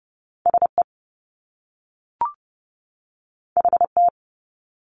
Morse code, heard as a keyed tone, about 30 wpm, 700 Hz. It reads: SE HT
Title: Top Two Letter Combinations in Words 30wpm